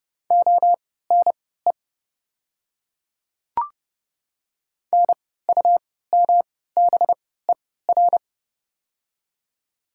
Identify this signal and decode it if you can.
Morse code, heard as a keyed tone, about 30 wpm, spaced as for 17 wpm, 700 Hz. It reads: ONE NUMBER
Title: 3x Character Spacing - Top 300 Words - 30wpm